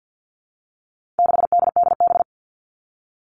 Morse code, heard as a keyed tone, about 48 wpm, 700 Hz. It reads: 6DDB
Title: All Cut Numbers - 4-Digit 50wpm